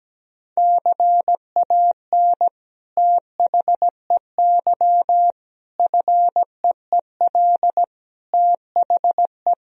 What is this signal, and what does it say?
Morse code, heard as a keyed tone, about 17 wpm, 700 Hz. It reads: CAN THEY FEEL THE
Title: Sentences from Top 1000 Words - Review Part 3 - 17wpm